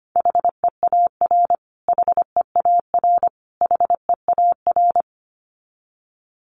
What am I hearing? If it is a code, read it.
Morse code, heard as a keyed tone, about 25 wpm, 700 Hz. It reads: HEAR HEAR HEAR
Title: Top 300 Words Repeated 25wpm